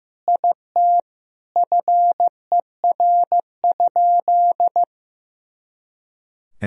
Morse code, heard as a keyed tone, about 15 wpm, 700 Hz. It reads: IT FER?
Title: Sentences from Top 100 Words and Abbreviations 15wpm